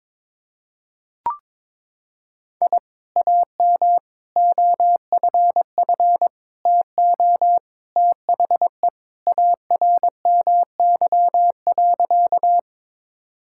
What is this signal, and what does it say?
Morse code, heard as a keyed tone, about 22 wpm, 700 Hz. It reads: I AM OFF TO THE ARMY.
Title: Sentences from Top 1000 Words - Review Part 5 - 22wpm